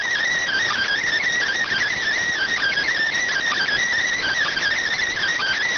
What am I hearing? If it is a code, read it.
Emergency Alert System (SAME) tones (header bursts), heard as harsh buzzing bursts. It is heard from 0.4 to 4.8 s.